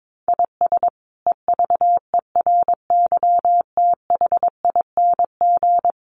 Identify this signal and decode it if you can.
Morse code, heard as a keyed tone, about 22 wpm, 700 Hz. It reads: IS EVERYTHING
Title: Sentences from Top 700 Words 22wpm